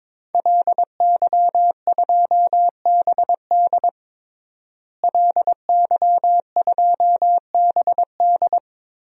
Morse code, heard as a keyed tone, about 22 wpm, 700 Hz. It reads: LY2BD LY2BD
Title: DXCC-100 Call Signs - 5 Characters - Repeated 3x - 22wpm